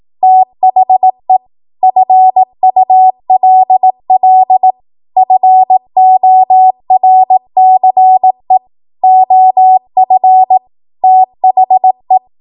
Morse code, heard as a keyed tone, about 18 wpm, 750 Hz. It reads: THE FULL FORCE OF THE
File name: Archive / 18 WPM / 18WPM